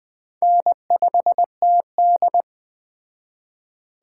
Morse code, heard as a keyed tone, about 19 wpm, 700 Hz. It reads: N5TD